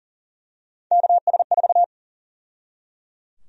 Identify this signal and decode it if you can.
Morse code, heard as a keyed tone, about 37 wpm, 700 Hz. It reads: KS4
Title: USA Call Sign Prefixes - 40wpm